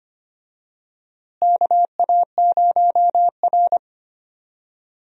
Morse code, heard as a keyed tone, about 25 wpm, 700 Hz. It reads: KA0R
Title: USA Call Signs - 4 Characters - Repeated 3x - 25wpm